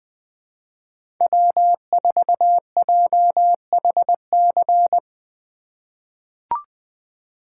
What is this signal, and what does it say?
Morse code, heard as a keyed tone, about 20 wpm, 700 Hz. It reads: W4JHC
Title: USA Call Signs - 5 Characters - 2x Character Spacing - 20wpm